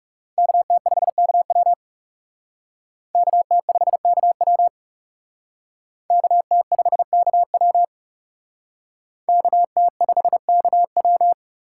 Morse code, heard as a keyed tone, about 37 wpm, 700 Hz. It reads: KT5KW KT5KW KT5KW KT5KW
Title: Speed Racing - USA Call Signs 30wpm